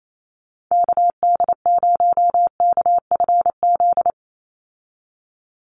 Morse code, heard as a keyed tone, about 28 wpm, 700 Hz. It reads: KD0KFZ
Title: USA Call Signs 28wpm